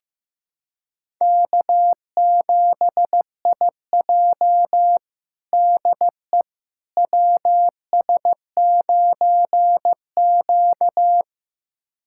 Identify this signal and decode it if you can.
Morse code, heard as a keyed tone, about 15 wpm, 700 Hz. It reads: K7IJ DE WS9Q
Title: USA Callsign DE Callsign - 4 Characters - 15wpm